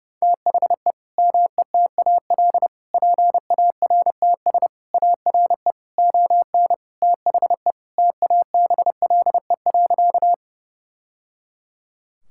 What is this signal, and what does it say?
Morse code, heard as a keyed tone, about 30 wpm, 700 Hz. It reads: THE METAL PARTS ARE ON THE TABLE.
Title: Sentences from Top 1000 Words - Review Part 6 - 30wpm